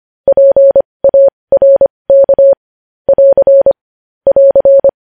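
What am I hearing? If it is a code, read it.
Morse code, heard as a keyed tone, about 25 wpm, spaced as for 20 wpm, 550 Hz. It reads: PARK <AR> <AR>
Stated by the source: Morse code news headlines practice at 20 wpm.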